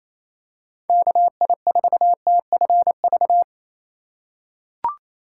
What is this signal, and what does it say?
Morse code, heard as a keyed tone, about 28 wpm, 700 Hz. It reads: KI4TFV
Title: USA Call Signs - 6 Characters - 4x Character Spacing - 28wpm